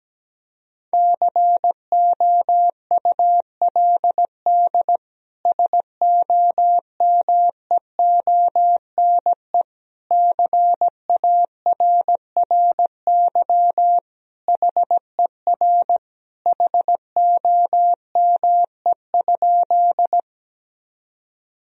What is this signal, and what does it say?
Morse code, heard as a keyed tone, about 17 wpm, 700 Hz. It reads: COULD SOMEONE CARRY HER HOME?
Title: Sentences from Top 1000 Words - Review Part 4 - 17wpm